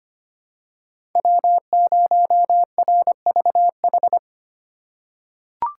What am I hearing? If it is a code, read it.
Morse code, heard as a keyed tone, about 25 wpm, 700 Hz. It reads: W0RVH
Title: USA Call Signs - 5 Characters - 4x Character Spacing - 25wpm